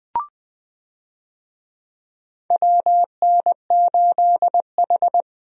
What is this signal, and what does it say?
Morse code, heard as a keyed tone, about 21 wpm, 700 Hz. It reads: WN8H
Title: USA Call Signs - 4 Characters - 20wpm